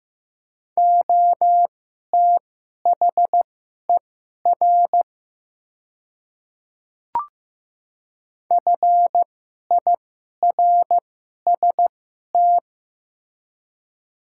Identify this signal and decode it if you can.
Morse code, heard as a keyed tone, about 15 wpm, spaced as for 11 wpm, 700 Hz. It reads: OTHER FIRST
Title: 2x Character Spacing - Top 100 Words - 15wpm